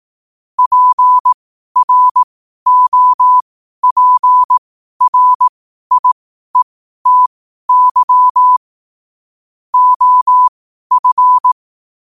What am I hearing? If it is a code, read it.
Morse code, heard as a keyed tone, about 18 wpm, spaced as for 12 wpm, 1000 Hz. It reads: PROPRIETY OF
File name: qotdfiles / 12wpm